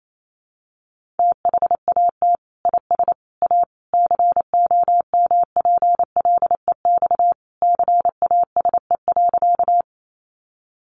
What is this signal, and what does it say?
Morse code, heard as a keyed tone, about 28 wpm, 700 Hz. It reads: THAT IS A COMPLEX CASE.
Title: Sentences from Top 900 Words 28wpm